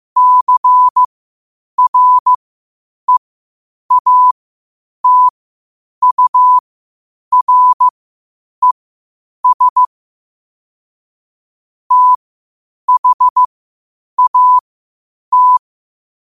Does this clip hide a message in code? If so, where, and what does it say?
Yes: Morse code from 0.2 to 15.6 s, heard as a keyed tone, about 15 wpm, spaced as for 8 wpm, 1000 Hz. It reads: CREATURES THAT